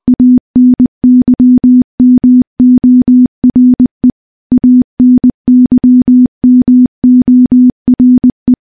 Morse code, heard as a keyed tone, about 21 wpm, 250 Hz. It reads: ANYMORE ANYMORE
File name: morse / morse-training-20WPM